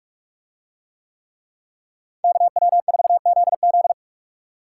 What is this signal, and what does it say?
Morse code, heard as a keyed tone, about 46 wpm, 700 Hz. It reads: KW4ZZ